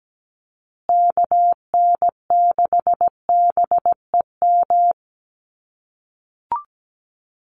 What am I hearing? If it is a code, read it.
Morse code, heard as a keyed tone, about 17 wpm, 700 Hz. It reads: KN6BEM
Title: USA Call Signs Repeated 3x 17wpm